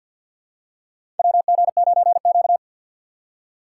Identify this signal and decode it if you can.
Morse code, heard as a keyed tone, about 48 wpm, 700 Hz. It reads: WG9Q
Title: USA Call Signs - 4 Characters - 6x Character Spacing - 50wpm